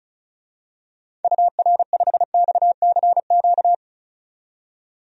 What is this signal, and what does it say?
Morse code, heard as a keyed tone, about 34 wpm, 700 Hz. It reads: UR5XCQ
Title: DXCC-100 Call Signs - 6 Characters - 35wpm